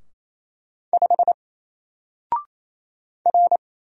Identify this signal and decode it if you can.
Morse code, heard as a keyed tone, about 27 wpm, 700 Hz. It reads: 5 R
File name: Single-Letter-Number-28wpm